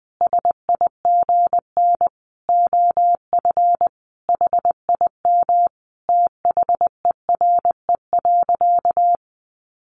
Morse code, heard as a keyed tone, about 20 wpm, 700 Hz. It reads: SIGN OF HIM THERE.
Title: Sentences from Top 1000 Words 20wpm